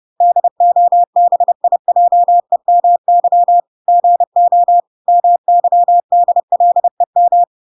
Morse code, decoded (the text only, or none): DOBIJEMY GO MYDLEM